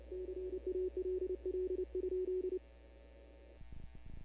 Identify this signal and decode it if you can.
Morse code, heard as a keyed tone, about 28 wpm, 370 Hz. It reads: CALL?
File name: CW / 2005-k5tr arrl cw dx -15:00